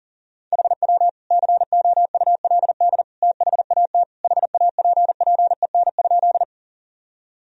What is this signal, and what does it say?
Morse code, heard as a keyed tone, about 40 wpm, 700 Hz. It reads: HW COULD THAT HAPPEN?
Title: Sentences from Top 400 Words and Abbreviations 40wpm